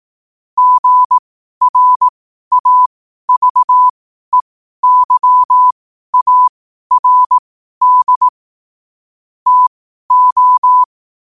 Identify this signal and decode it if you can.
Morse code, heard as a keyed tone, about 18 wpm, spaced as for 12 wpm, 1000 Hz. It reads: GRAVEYARD TO
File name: qotdfiles / 12wpm